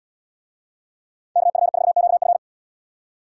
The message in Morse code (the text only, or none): KF4QF